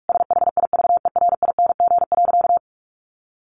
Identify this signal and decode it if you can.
Morse code, heard as a keyed tone, about 44 wpm, 700 Hz. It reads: SHIVERING.
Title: A Sound Of Thunder 45wpm